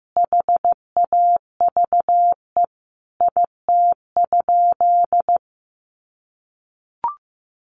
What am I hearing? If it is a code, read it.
Morse code, heard as a keyed tone, about 15 wpm, 700 Hz. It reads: HAVE IT?
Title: Sentences from Top 100 Words 15wpm